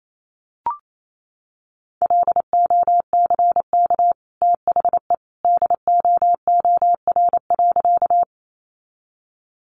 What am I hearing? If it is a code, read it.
Morse code, heard as a keyed tone, about 28 wpm, 700 Hz. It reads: LOCK THE DOOR.
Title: Sentences from Top 1300 Words 28wpm